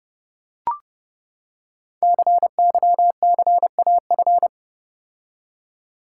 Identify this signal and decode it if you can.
Morse code, heard as a keyed tone, about 28 wpm, 700 Hz. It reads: CYCAF
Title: Code Groups - 5 Letters - 30wpm